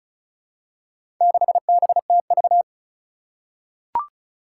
Morse code, heard as a keyed tone, about 36 wpm, 700 Hz. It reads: 6BTV
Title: Four Challenging Letter-Number Combinations 35wpm